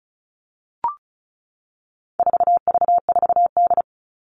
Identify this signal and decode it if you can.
Morse code, heard as a keyed tone, about 34 wpm, 700 Hz. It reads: 4V4D
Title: All Cut Numbers - 4-Digit 35wpm